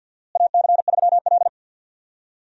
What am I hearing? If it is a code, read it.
Morse code, heard as a keyed tone, about 52 wpm, 700 Hz. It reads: AC3L
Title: USA Call Signs - 4 Characters - Repeated 3x - 50wpm